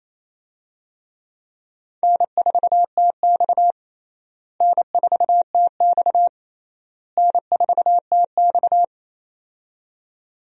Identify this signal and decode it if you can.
Morse code, heard as a keyed tone, about 28 wpm, 700 Hz. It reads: N4TX N4TX N4TX
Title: USA Call Signs - 4 Characters - Repeated 3x - 28wpm